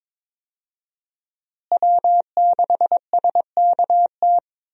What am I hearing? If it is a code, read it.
Morse code, heard as a keyed tone, about 22 wpm, 700 Hz. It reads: W6SKT
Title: USA Call Signs - 5 Characters - Repeated 2x - 22wpm